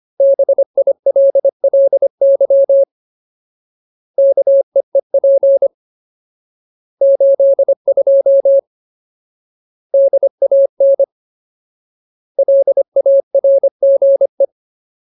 Morse code, heard as a keyed tone, about 25 wpm, 550 Hz. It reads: BILLY KEEP 82 DAN LARGE
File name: Ultimate Word List / 25WPM Ultimate